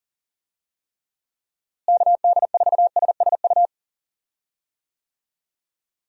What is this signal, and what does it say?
Morse code, heard as a keyed tone, about 40 wpm, 700 Hz. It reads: KD4SSU
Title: USA Call Signs - 6 Characters - Repeated 3x - 40wpm